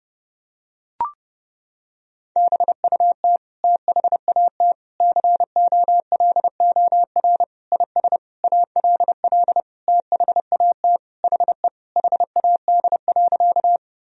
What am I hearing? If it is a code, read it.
Morse code, heard as a keyed tone, about 32 wpm, 700 Hz. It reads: BUT THAT COLOR IS ALL THAT HE HAD.
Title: Sentences from Top 1000 Words - Review Part 2 - 30wpm